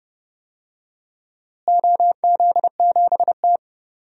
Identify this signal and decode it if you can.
Morse code, heard as a keyed tone, about 30 wpm, 700 Hz. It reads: OZ7T
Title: DXCC-100 Call Signs - 4 Characters - Repeated 3x - 30wpm